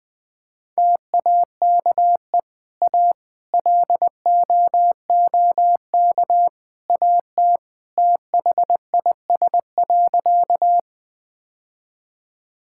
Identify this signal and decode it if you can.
Morse code, heard as a keyed tone, about 20 wpm, 700 Hz. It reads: TAKE A LOOK AT THIS.
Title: Sentences with 3-Word Phrases - 20wpm